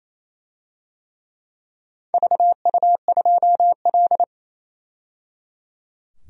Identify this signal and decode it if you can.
Morse code, heard as a keyed tone, about 28 wpm, 700 Hz. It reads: VU2L
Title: DXCC-100 Call Signs - 4 Characters - 28wpm